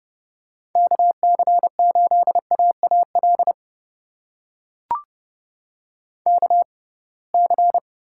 Morse code, heard as a keyed tone, about 31 wpm, spaced as for 23 wpm, 700 Hz. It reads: KC8AAL KC
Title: USA Call Signs - 6 Characters - 6x Character Spacing - 30wpm